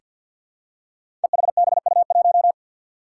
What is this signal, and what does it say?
Morse code, heard as a keyed tone, about 47 wpm, 700 Hz. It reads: EHBU1
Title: Five Challenging Letter-Number Combinations 50wpm